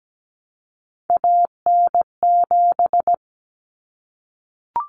Morse code, heard as a keyed tone, about 17 wpm, 700 Hz. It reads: AN7
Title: Common Cut Numbers - 3-Digit 17wpm